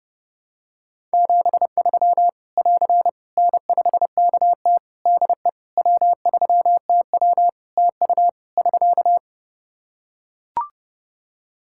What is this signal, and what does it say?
Morse code, heard as a keyed tone, about 30 wpm, 700 Hz. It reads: 73 <AR> N5KT DE W3TW TU <SK>
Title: Practice QSO - Exchange 4 - Ending - 30wpm